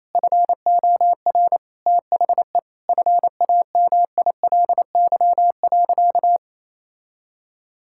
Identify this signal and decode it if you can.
Morse code, heard as a keyed tone, about 28 wpm, 700 Hz. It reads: FOR THE FAMILY.
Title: Sentences from Top 1000 Words - Review Part 5 - 28wpm